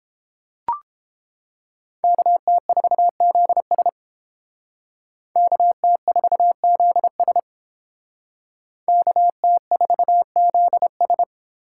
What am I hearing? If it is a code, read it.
Morse code, heard as a keyed tone, about 31 wpm, 700 Hz. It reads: KT4ZS KT4ZS KT4ZS